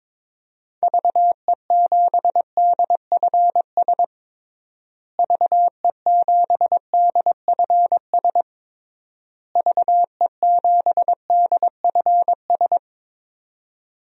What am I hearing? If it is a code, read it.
Morse code, heard as a keyed tone, about 21 wpm, 700 Hz. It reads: VE7DFS VE7DFS VE7DFS
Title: DXCC-100 Call Signs - 6 Characters - Repeated 3x - 22wpm